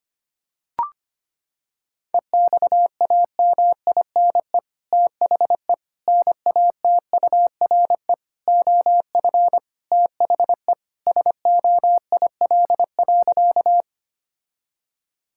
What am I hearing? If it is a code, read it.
Morse code, heard as a keyed tone, about 25 wpm, 700 Hz. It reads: EXAMINE THE NATURE OF THE SOIL.
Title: Sentences from Top 1600 Words 25wpm